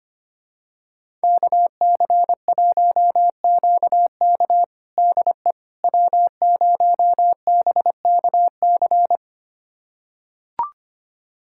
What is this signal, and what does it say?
Morse code, heard as a keyed tone, about 25 wpm, 700 Hz. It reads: KC1QK DE W0BKC
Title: USA Callsign DE Callsign - 5 Characters - 25wpm